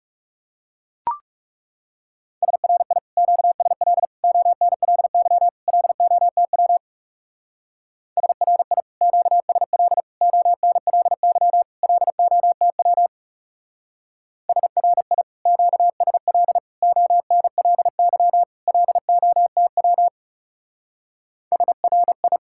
Morse code, heard as a keyed tone, about 35 wpm, 700 Hz. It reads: SRI QSL ONLY LOTW SRI QSL ONLY LOTW SRI QSL ONLY LOTW SRI
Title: Speed-Racing Common QSO Phrases - 30wpm